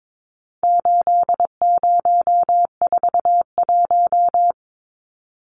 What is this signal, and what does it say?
Morse code, heard as a keyed tone, about 22 wpm, 700 Hz. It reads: 8041
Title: Numbers - 4-Digit 22wpm